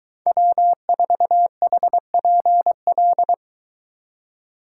Morse code, heard as a keyed tone, about 23 wpm, 700 Hz. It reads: W4HPL